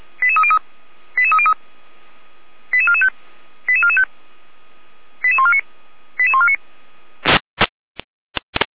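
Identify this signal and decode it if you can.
Five-tone selective call (ZVEI), heard as quick runs of five tones, tones of about 70 ms. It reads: 88303 88303 88445 88445 88149 88149
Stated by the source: ZVEI five-tone selective call received on an SDR.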